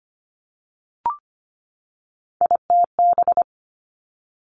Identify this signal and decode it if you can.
Morse code, heard as a keyed tone, about 25 wpm, 700 Hz. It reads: ITB
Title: Three Challenging Letter-Number Combinations 25wpm